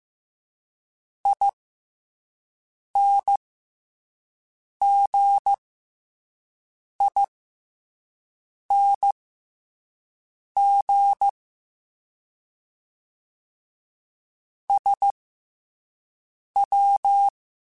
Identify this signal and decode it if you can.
Morse code, heard as a keyed tone, about 15 wpm, spaced as for 5 wpm, 800 Hz. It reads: INGING SW